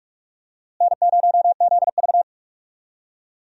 Morse code, heard as a keyed tone, about 41 wpm, 700 Hz. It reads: N0ZV
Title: USA Call Signs - 4 Characters - Repeated 2x - 45wpm